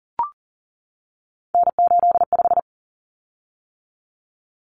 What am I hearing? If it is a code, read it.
Morse code, heard as a keyed tone, about 40 wpm, 700 Hz. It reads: N85
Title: Common Cut Numbers - 3-Digit 40wpm